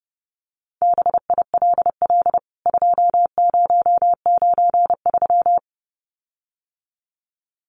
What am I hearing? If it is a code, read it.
Morse code, heard as a keyed tone, about 28 wpm, 700 Hz. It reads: BILL 2093
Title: CWT Exchange 30wpm